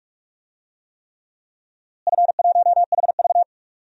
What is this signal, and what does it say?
Morse code, heard as a keyed tone, about 46 wpm, 700 Hz. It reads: F1HV
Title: DXCC-100 Call Signs - 4 Characters - Repeated 3x - 45wpm